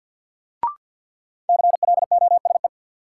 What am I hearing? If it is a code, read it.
Morse code, heard as a keyed tone, about 52 wpm, 700 Hz. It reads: CLOSE
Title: Sentences from Top 300 Words - New Word Drill - 50wpm